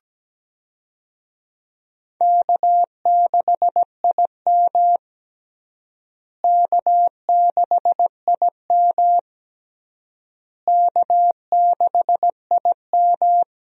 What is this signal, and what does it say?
Morse code, heard as a keyed tone, about 17 wpm, 700 Hz. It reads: K6IM K6IM K6IM
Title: USA Call Signs - 4 Characters - Repeated 3x - 17wpm